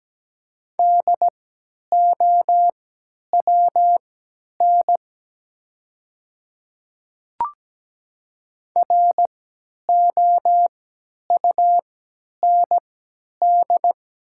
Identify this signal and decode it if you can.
Morse code, heard as a keyed tone, about 17 wpm, spaced as for 10 wpm, 700 Hz. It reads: DOWN ROUND
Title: 3x Character Spacing - Top 200 Words - 17wpm